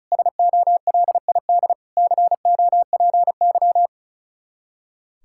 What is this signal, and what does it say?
Morse code, heard as a keyed tone, about 42 wpm, 700 Hz. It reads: SOLID COPY
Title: Speed-Racing Common QSO Phrases - 35wpm